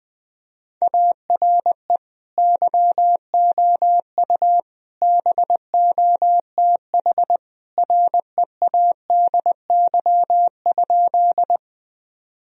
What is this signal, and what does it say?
Morse code, heard as a keyed tone, about 20 wpm, 700 Hz. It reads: ARE YOU BOTH READY?
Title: Sentences from Top 1000 Words - Review Part 1 - 20wpm